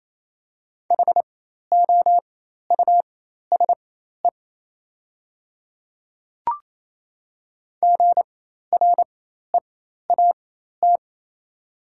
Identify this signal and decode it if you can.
Morse code, heard as a keyed tone, about 28 wpm, spaced as for 13 wpm, 700 Hz. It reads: HOUSE GREAT
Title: 4x Character Spacing - Top 200 Words - 28wpm